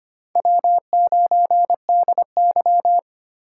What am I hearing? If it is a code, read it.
Morse code, heard as a keyed tone, about 25 wpm, 700 Hz. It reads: W9DY
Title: K1USN SST QSO Practice - 25wpm